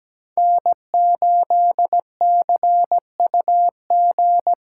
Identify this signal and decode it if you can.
Morse code, heard as a keyed tone, about 17 wpm, 700 Hz. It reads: N8CUG
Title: USA Call Signs - 4x Character Spacing - 17wpm